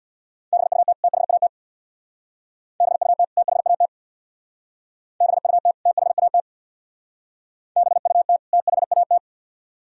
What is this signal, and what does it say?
Morse code, heard as a keyed tone, about 63 wpm, 700 Hz. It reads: BUT THAT BUT THAT BUT THAT BUT THAT